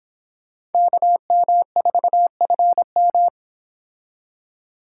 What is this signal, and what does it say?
Morse code, heard as a keyed tone, about 26 wpm, 700 Hz. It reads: KM4FM